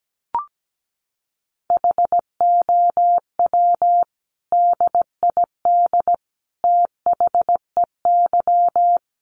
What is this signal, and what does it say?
Morse code, heard as a keyed tone, about 17 wpm, 700 Hz. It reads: HOW DID THEY
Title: Sentences from Top 1100 Words 17wpm